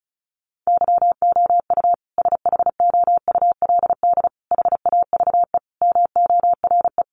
Morse code, heard as a keyed tone, about 34 wpm, 700 Hz. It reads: YOU SHOULD HAVE MORE
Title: Sentences from Top 1300 Words 35wpm